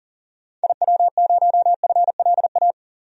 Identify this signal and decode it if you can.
Morse code, heard as a keyed tone, about 38 wpm, 700 Hz. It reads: IW0FLA
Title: DXCC-100 Call Signs - 6 Characters - Repeated 3x - 40wpm